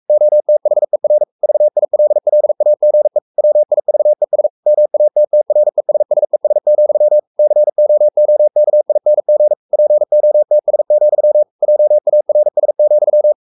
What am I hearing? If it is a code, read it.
Morse code, heard as a keyed tone, about 44 wpm, 600 Hz. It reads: OTHER VILLAGE WIVES MATTRESSES, COOKING POTS, JARS,
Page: Romanen Never af Ken Follet på engelsk.